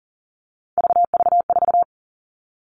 Morse code, heard as a keyed tone, about 40 wpm, 700 Hz. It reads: VV4
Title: All Cut Numbers - 3-Digit 40wpm